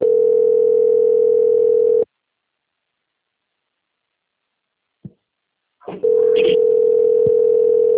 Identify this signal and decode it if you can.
Ringback tone (North America), 440 + 480 Hz.
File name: out- - - - - .4982